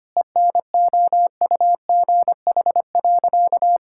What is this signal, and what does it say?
Morse code, heard as a keyed tone, about 25 wpm, 700 Hz. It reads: ENOUGH.
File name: practice / 25wpm / Sentences with 4-Word Phrases - 25wpm